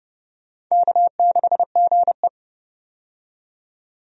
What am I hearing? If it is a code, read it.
Morse code, heard as a keyed tone, about 29 wpm, 700 Hz. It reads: K6GE
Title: USA Call Signs - 4 Characters - 30wpm